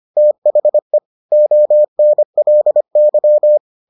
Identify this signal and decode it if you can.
Morse code, heard as a keyed tone, about 25 wpm, 600 Hz. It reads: THE ONLY